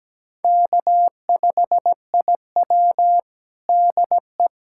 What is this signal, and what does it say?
Morse code, heard as a keyed tone, about 17 wpm, 700 Hz. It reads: K5IW DE
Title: USA Callsign DE Callsign - 4 Characters - 17wpm